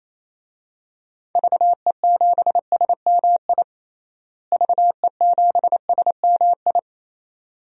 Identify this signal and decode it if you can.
Morse code, heard as a keyed tone, about 28 wpm, 700 Hz. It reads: VE7SMI VE7SMI
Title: DXCC-100 Call Signs - 6 Characters - Repeated 2x - 28wpm